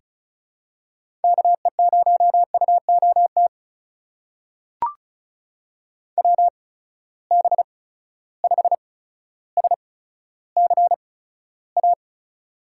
Morse code, heard as a keyed tone, about 36 wpm, spaced as for 15 wpm, 700 Hz. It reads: KE0UOT WB5SCA
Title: USA Call Signs - 6 Characters - 8x Character Spacing - 35wpm